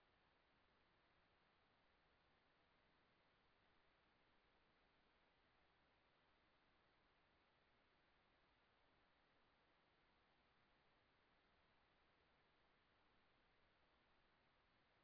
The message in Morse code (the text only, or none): none